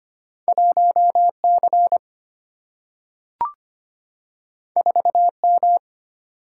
Morse code, heard as a keyed tone, about 25 wpm, 700 Hz. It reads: 1C 4M
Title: Single-Letter Single-Number 25wpm